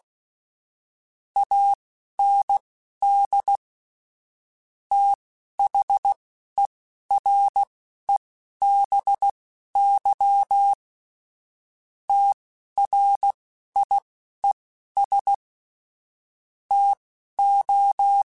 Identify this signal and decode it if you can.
Morse code, heard as a keyed tone, about 16 wpm, spaced as for 11 wpm, 800 Hz. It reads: AND THEREBY TRIES TO
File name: Technical Subjects / Technical Subjects-mp3s / XL @10